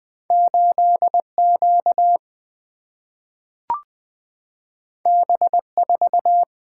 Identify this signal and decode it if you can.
Morse code, heard as a keyed tone, about 20 wpm, 700 Hz. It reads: 8 Q B 4
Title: Single-Letter Single-Number 20wpm